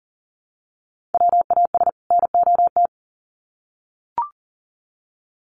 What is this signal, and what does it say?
Morse code, heard as a keyed tone, about 40 wpm, 700 Hz. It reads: WAS NOT
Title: Sets of 2 Words 40wpm